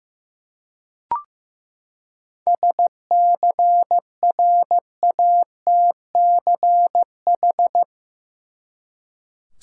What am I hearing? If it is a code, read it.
Morse code, heard as a keyed tone, about 15 wpm, 700 Hz. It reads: SCRATCH
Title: Seven Letter Words - 15wpm